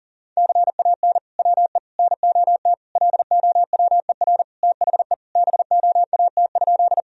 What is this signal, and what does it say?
Morse code, heard as a keyed tone, about 40 wpm, 700 Hz. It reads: CAN WE NOT LOWER THE BOAT?
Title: Sentences from Top 1000 Words - Review Part 4 - 40wpm